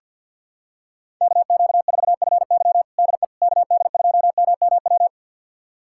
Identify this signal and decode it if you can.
Morse code, heard as a keyed tone, about 52 wpm, 700 Hz. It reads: KQ4FY DE KD2MMW
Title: USA Callsign DE Callsign 50wpm